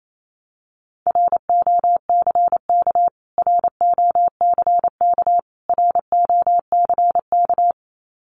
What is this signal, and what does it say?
Morse code, heard as a keyed tone, about 28 wpm, 700 Hz. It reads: ROCK ROCK ROCK
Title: Top 500 Words Repeated 28wpm